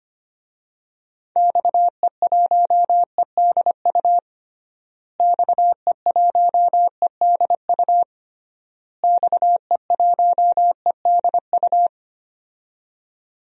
Morse code, heard as a keyed tone, about 25 wpm, 700 Hz. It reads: XE1EDU XE1EDU XE1EDU
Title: DXCC-100 Call Signs - 6 Characters - Repeated 3x - 25wpm